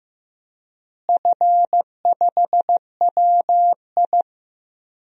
Morse code, heard as a keyed tone, about 15 wpm, 700 Hz. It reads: F5WI
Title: DXCC-100 Call Signs - 4 Characters - Repeated 3x - 15wpm